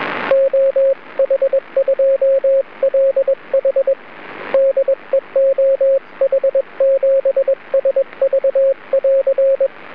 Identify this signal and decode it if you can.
Morse code, heard as a keyed tone, about 21 wpm, 540 Hz, with radio noise. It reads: OH2LH DE OH7SV<AR>
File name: juma-tx1 / juma-tx1-cw